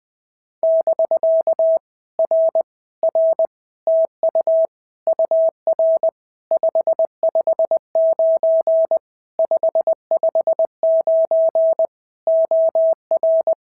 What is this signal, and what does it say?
Morse code, heard as a keyed tone, about 20 wpm, 650 Hz. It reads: <BK> R R TU UR 559 559 OR
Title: POTA QSO Practice - 20wpm